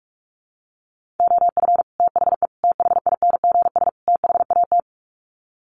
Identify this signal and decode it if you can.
Morse code, heard as a keyed tone, about 44 wpm, 700 Hz. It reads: OF THE THINGS THAT
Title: Sets of 4 Words 45wpm